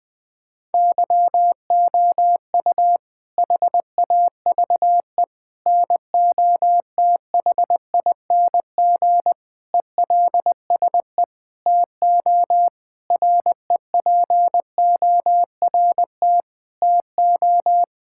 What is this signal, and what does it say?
Morse code, heard as a keyed tone, about 21 wpm, 700 Hz. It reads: YOU HAVE NOTHING ELSE TO REPORT TO
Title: Sentences from Top 1000 Words - Review Part 6 - 20wpm